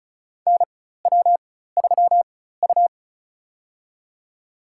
Morse code, heard as a keyed tone, about 36 wpm, spaced as for 16 wpm, 700 Hz. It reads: NW3U
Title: USA Call Signs - 4 Characters - 4x Character Spacing - 35wpm